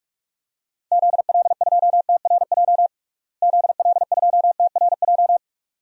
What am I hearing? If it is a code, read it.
Morse code, heard as a keyed tone, about 43 wpm, 700 Hz. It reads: ZL2TRJ ZL2TRJ
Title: DXCC-100 Call Signs - 6 Characters - Repeated 2x - 45wpm